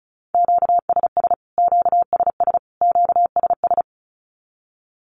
Morse code, heard as a keyed tone, about 34 wpm, 700 Hz. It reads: QSS QSS QSS
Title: Q-Codes Repeated 3x 35wpm